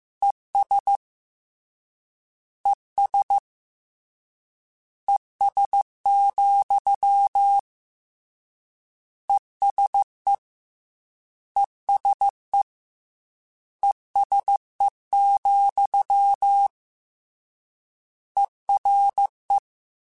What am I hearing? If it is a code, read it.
Morse code, heard as a keyed tone, about 15 wpm, 800 Hz. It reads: ES ES ES, ESE ESE ESE, ERE